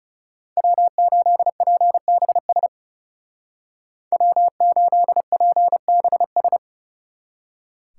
Morse code, heard as a keyed tone, about 37 wpm, 700 Hz. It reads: W8PBS W8PBS
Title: Speed Racing - USA 1x3 Call Signs 30wpm